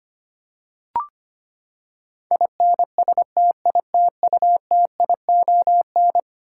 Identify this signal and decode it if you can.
Morse code, heard as a keyed tone, about 25 wpm, 700 Hz. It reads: INSTITUTION
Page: morse-code-ninja-practice-25wpm